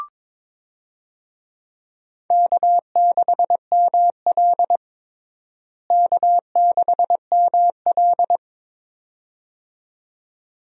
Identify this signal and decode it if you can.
Morse code, heard as a keyed tone, about 22 wpm, 700 Hz. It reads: K6ML K6ML
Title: USA Call Signs - 4 Characters - Repeated 2x - 22wpm